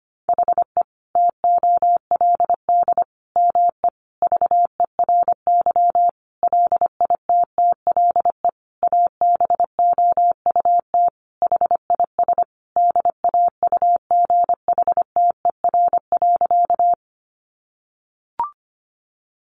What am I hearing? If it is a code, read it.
Morse code, heard as a keyed tone, about 25 wpm, 700 Hz. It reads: HE TOLD ME VERY LITTLE ABOUT HIS DAUGHTER.
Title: Sentences from Top 1500 Words 25wpm